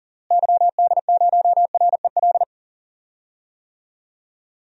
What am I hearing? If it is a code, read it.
Morse code, heard as a keyed tone, about 38 wpm, 700 Hz. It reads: YD0REL
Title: DXCC-100 Call Signs - 6 Characters - Repeated 2x - 40wpm